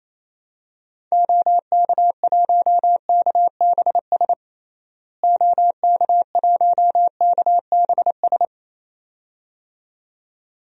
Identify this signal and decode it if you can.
Morse code, heard as a keyed tone, about 28 wpm, 700 Hz. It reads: OK1KBS OK1KBS
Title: DXCC-100 Call Signs - 6 Characters - Repeated 2x - 28wpm